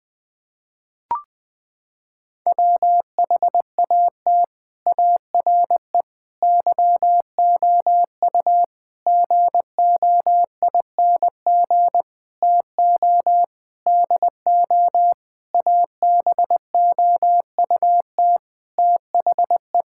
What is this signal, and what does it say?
Morse code, heard as a keyed tone, about 21 wpm, 700 Hz. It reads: WHAT ARE YOU GOING TO DO ABOUT THE
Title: Sentences with 8-Word Phrases - 20wpm